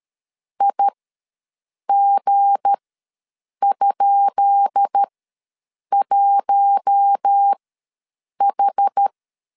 Morse code, heard as a keyed tone, about 13 wpm, spaced as for 7 wpm, 800 Hz. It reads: IG?1H
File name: LEZIONI / 10 SIGNS / 10B INT ws 12wpm 3527